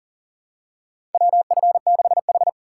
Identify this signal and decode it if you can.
Morse code, heard as a keyed tone, about 40 wpm, 700 Hz. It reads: WF6H
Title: USA Call Signs - 4 Characters - 6x Character Spacing - 40wpm